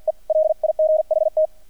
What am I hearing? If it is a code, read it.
Morse code, heard as a keyed tone, about 48 wpm, 620 Hz. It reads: E LIGHT
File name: archived / CRSnet